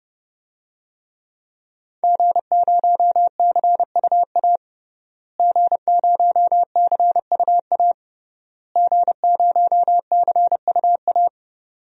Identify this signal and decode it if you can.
Morse code, heard as a keyed tone, about 30 wpm, 700 Hz. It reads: G0CUA G0CUA G0CUA
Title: DXCC-100 Call Signs - 5 Characters - Repeated 3x - 30wpm